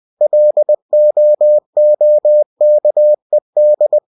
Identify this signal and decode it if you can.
Morse code, heard as a keyed tone, about 20 wpm, 600 Hz. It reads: LOOKED
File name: cw-books / winnie-the-pooh / winnie-20wpm-page-0058